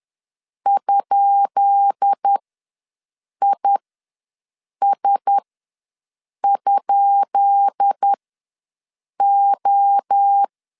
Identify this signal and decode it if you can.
Morse code, heard as a keyed tone, about 10 wpm, spaced as for 6 wpm, 800 Hz. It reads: ?IS?O
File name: LEZIONI / 10 SIGNS / 10B INT ws 10wpm 4134